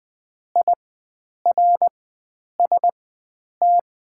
Morse code, heard as a keyed tone, about 20 wpm, spaced as for 9 wpm, 690 Hz. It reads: IRST
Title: 4x Character Spacing - Top 100 Words - 20wpm